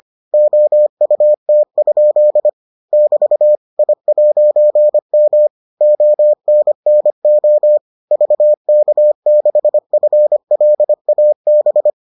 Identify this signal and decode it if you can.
Morse code, heard as a keyed tone, about 25 wpm, 600 Hz. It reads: OUT? <BT> I'M ONNO VK6FLAB